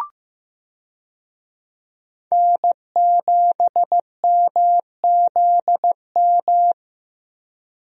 Morse code, heard as a keyed tone, about 15 wpm, 700 Hz. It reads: N7MZM
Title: USA Call Signs - 5 Characters - Repeated 3x - 15wpm